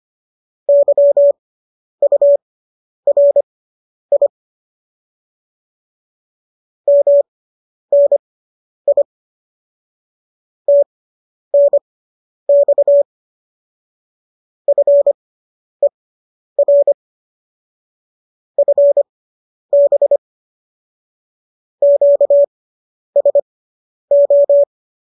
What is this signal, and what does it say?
Morse code, heard as a keyed tone, about 25 wpm, spaced as for 10 wpm, 570 Hz. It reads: YURI MNI TNX FER FB QSO